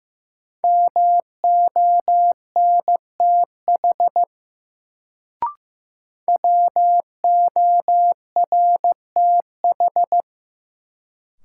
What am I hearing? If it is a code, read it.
Morse code, heard as a keyed tone, about 15 wpm, 700 Hz. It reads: MONTH WORTH
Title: Sentences from Top 800 Words - New Word Drill - 15wpm